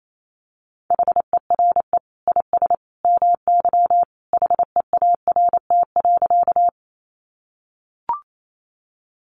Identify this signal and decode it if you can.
Morse code, heard as a keyed tone, about 28 wpm, 700 Hz. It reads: HERE IS MY HEART.
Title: Sentences from Top 1200 Words 28wpm